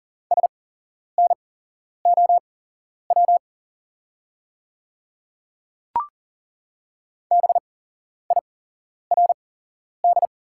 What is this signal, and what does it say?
Morse code, heard as a keyed tone, about 40 wpm, spaced as for 11 wpm, 690 Hz. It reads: SNOW BIRD
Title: 8x Character Spacing - Top 500 Words - 40wpm